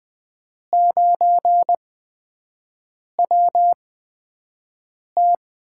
Morse code, heard as a keyed tone, about 20 wpm, spaced as for 5 wpm, 700 Hz. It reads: 9WT
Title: USA Call Signs - 4 Characters - 8x Character Spacing - 20wpm